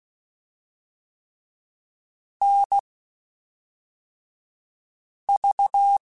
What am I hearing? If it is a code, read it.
Morse code, heard as a keyed tone, about 16 wpm, spaced as for 3 wpm, 800 Hz. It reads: NV
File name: QSOs / QSOs-mp3s / WD0FBT 1@5